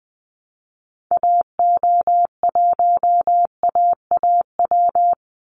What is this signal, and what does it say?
Morse code, heard as a keyed tone, about 20 wpm, 700 Hz. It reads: AO1AAW
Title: Call Signs - 20wpm